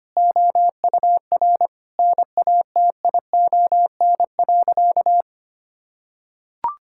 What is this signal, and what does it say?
Morse code, heard as a keyed tone, about 25 wpm, 700 Hz. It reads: OUR NATION.
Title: Sentences from Top 1000 Words - Review Part 6 - 25wpm